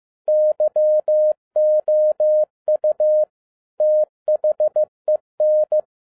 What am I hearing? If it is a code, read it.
Morse code, heard as a keyed tone, about 15 wpm, 610 Hz. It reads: YOU THEN
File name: 2011-10-mp3 / n12-JL3YMV 15wpm